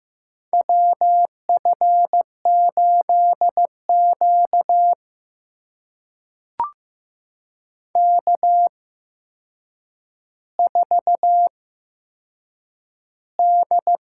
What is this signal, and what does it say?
Morse code, heard as a keyed tone, about 15 wpm, spaced as for 7 wpm, 700 Hz. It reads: WF8Q K4D
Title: USA Call Signs - 4 Characters - 8x Character Spacing - 15wpm